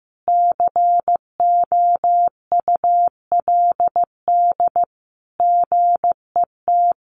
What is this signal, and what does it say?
Morse code, heard as a keyed tone, about 15 wpm, 700 Hz. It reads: COULD GET
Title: Sentences from Top 1400 Words 15wpm